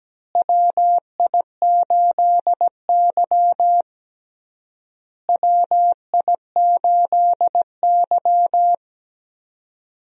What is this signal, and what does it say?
Morse code, heard as a keyed tone, about 17 wpm, 700 Hz. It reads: WI8Y WI8Y
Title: USA Call Signs - 4 Characters - Repeated 3x - 17wpm